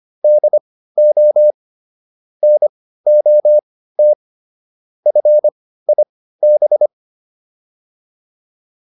Morse code, heard as a keyed tone, about 25 wpm, spaced as for 15 wpm, 600 Hz. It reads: DO NOT FIB